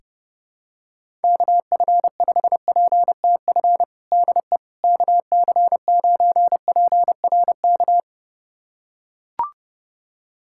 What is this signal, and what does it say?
Morse code, heard as a keyed tone, about 28 wpm, 700 Hz. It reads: KF5PTF DE KC9PRK
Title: USA Callsign DE Callsign - 6 Characters - 30wpm